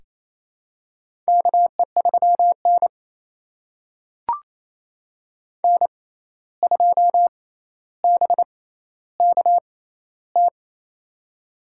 Morse code, heard as a keyed tone, about 28 wpm, spaced as for 13 wpm, 700 Hz. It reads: KE3N N2BKT
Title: USA Call Signs - 6x Character Spacing - 28wpm